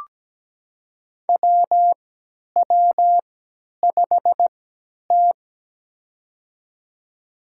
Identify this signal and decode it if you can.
Morse code, heard as a keyed tone, about 17 wpm, spaced as for 10 wpm, 700 Hz. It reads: WW5T
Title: USA Call Signs - 4 Characters - 3x Character Spacing - 17wpm